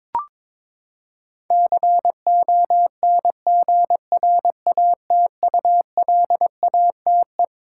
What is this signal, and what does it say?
Morse code, heard as a keyed tone, about 22 wpm, 700 Hz. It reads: CONGRATULATE
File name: Twelve Letter Words - 22wpm